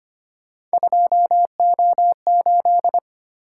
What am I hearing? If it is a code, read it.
Morse code, heard as a keyed tone, about 25 wpm, 700 Hz. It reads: 2O8
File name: Three Letter-Number Combinations - 25wpm